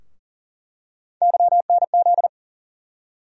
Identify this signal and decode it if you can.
Morse code, heard as a keyed tone, about 40 wpm, 700 Hz. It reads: YNZ
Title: USA Call Sign Postfixes - 3 Letters 40wpm